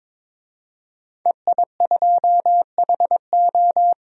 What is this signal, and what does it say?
Morse code, heard as a keyed tone, about 22 wpm, 700 Hz. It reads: EI2HO
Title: DXCC-100 Call Signs - 5 Characters - 22wpm